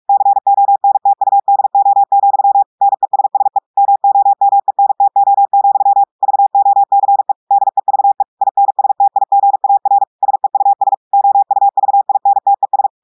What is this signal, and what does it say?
Morse code, heard as a keyed tone, about 46 wpm, 800 Hz. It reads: CONTUDO, NESSE MOMENTO, VOCE DEVE INSTIGAR SEUS OUVINTES
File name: audios / portugais / portugais texte en clair 2 vitesse 45 mots minutes